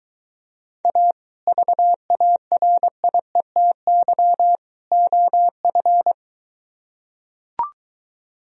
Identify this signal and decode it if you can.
Morse code, heard as a keyed tone, about 23 wpm, 700 Hz. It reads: A VARIETY OF